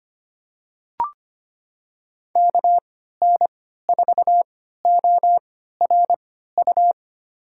Morse code, heard as a keyed tone, about 25 wpm, spaced as for 14 wpm, 700 Hz. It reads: KN4ORU
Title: USA Call Signs - 6 Characters - 3x Character Spacing - 25wpm